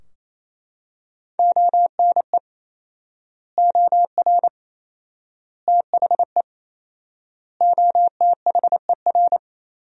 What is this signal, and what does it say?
Morse code, heard as a keyed tone, about 28 wpm, 700 Hz. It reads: ONE OR THE OTHER
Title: 4x Word-Spacing - Sentences from Top 400 Words - 28wpm